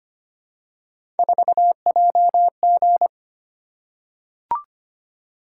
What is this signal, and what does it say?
Morse code, heard as a keyed tone, about 25 wpm, 700 Hz. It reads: 4JG
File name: Three Letter-Number Combinations - 25wpm